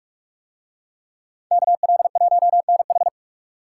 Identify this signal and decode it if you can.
Morse code, heard as a keyed tone, about 44 wpm, 700 Hz. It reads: KL1NH